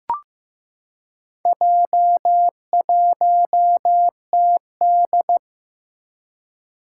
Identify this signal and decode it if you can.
Morse code, heard as a keyed tone, about 15 wpm, 700 Hz. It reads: J1TD
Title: Four Challenging Letter-Number Combinations 15wpm